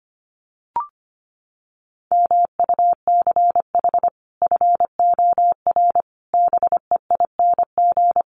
Morse code, heard as a keyed tone, about 25 wpm, 700 Hz. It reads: MUCH FOR BEING
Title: Sets of 5 Words 25wpm